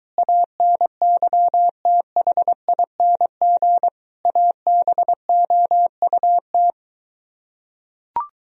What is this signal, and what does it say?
Morse code, heard as a keyed tone, about 23 wpm, 700 Hz. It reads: ANYTHING ABOUT